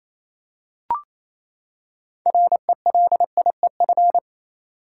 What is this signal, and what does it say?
Morse code, heard as a keyed tone, about 28 wpm, 700 Hz. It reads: RELIEF
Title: Six Letter Words - 28wpm